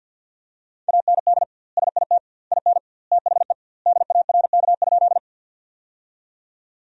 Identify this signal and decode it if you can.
Morse code, heard as a keyed tone, about 52 wpm, 700 Hz. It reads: AND SIT IN THE DARK?
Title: 2x Word-Spacing - Sentences from Top 500 Words - 50wpm